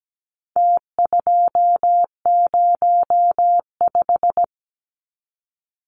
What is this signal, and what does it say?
Morse code, heard as a keyed tone, about 17 wpm, 700 Hz. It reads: T205
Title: Numbers - Contest Style - 4-Digit 17wpm